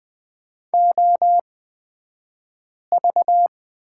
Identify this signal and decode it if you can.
Morse code, heard as a keyed tone, about 20 wpm, spaced as for 5 wpm, 700 Hz. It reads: OV
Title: 228 Lesson 37d - Letter x - Two Characters 20wpm